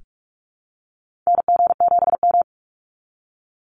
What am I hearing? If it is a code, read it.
Morse code, heard as a keyed tone, about 44 wpm, 700 Hz. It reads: NG7M
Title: Call Signs Repeated 3x - 45wpm